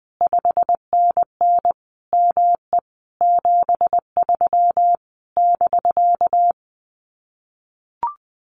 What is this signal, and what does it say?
Morse code, heard as a keyed tone, about 20 wpm, 700 Hz. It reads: 5NN ME 73 <BK>
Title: Common QSO Phrases Encore 20wpm